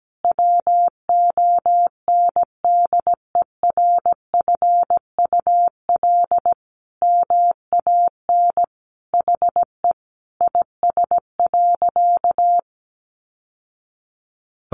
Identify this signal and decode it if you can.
Morse code, heard as a keyed tone, about 17 wpm, 700 Hz. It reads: WONDERFUL MAN HE IS.
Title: Sentences from Top 1100 Words 17wpm